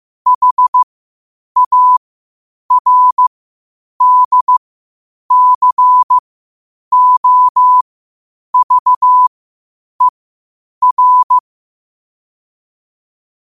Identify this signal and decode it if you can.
Morse code, heard as a keyed tone, about 15 wpm, spaced as for 8 wpm, 1000 Hz. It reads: HARDCOVER
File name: qotdfiles / 8wpm